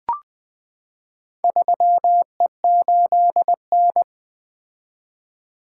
Morse code, heard as a keyed tone, about 20 wpm, 700 Hz. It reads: 3E8N
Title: Four Letter-Number Combinations - 20wpm